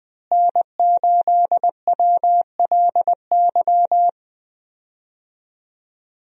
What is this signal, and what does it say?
Morse code, heard as a keyed tone, about 20 wpm, 700 Hz. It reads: N8WLY
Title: USA Call Signs - 5 Characters - Repeated 3x - 20wpm